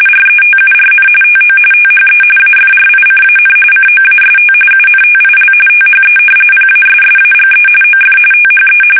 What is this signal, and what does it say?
RTTY, about 45.45 baud. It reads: RETURN TO THE AFRICAN COAST UNTIL THE PASSAGE OF THE
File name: fsk850 / usnavy2